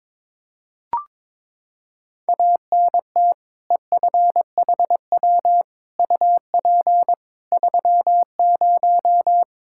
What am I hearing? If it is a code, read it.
Morse code, heard as a keyed tone, about 22 wpm, 700 Hz. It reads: ANT EFHW UP 30
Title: Practice QSO - Exchange 2 - Info - 22wpm